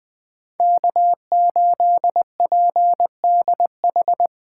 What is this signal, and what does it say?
Morse code, heard as a keyed tone, about 20 wpm, 700 Hz. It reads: K8PDH
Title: USA Call Signs - 3x Character Spacing - 20wpm